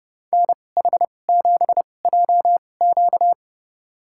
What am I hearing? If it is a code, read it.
Morse code, heard as a keyed tone, about 30 wpm, spaced as for 22 wpm, 700 Hz. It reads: NH7JQ
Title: USA Call Signs - 5 Characters - 2x Character Spacing - 30wpm